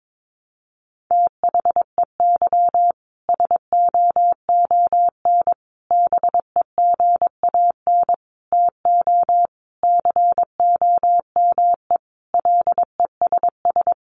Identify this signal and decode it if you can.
Morse code, heard as a keyed tone, about 22 wpm, 700 Hz. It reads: THEY SOON BEGAN TO COME LESS
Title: Sentences from Top 1300 Words 22wpm